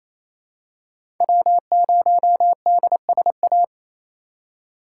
Morse code, heard as a keyed tone, about 28 wpm, 700 Hz. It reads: W0DSA
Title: USA Call Signs - 5 Characters - 6x Character Spacing - 28wpm